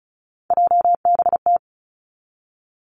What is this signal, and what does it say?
Morse code, heard as a keyed tone, about 36 wpm, 700 Hz. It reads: JBT
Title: Three Challenging Letter-Number Combinations 35wpm